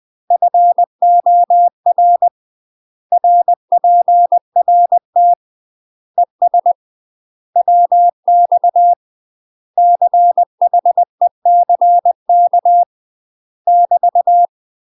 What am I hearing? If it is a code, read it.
Morse code, heard as a keyed tone, about 20 wpm, 700 Hz. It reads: FOR RPRT ES WX CHECK <BT>
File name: UK MASTER QSOs TRAINING / 20WPM / UK MASTER QSOs 0053